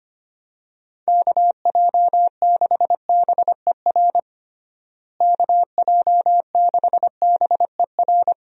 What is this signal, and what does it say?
Morse code, heard as a keyed tone, about 25 wpm, 700 Hz. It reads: KJ6BER KJ6BER
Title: USA Call Signs - 6 Characters - Repeated 3x - 25wpm